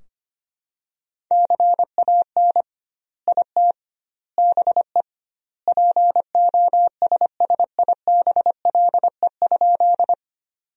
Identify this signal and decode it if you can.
Morse code, heard as a keyed tone, about 25 wpm, 700 Hz. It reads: CAN IT BE POSSIBLE?
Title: 2x Word-Spacing - Sentences from Top 500 Words - 25wpm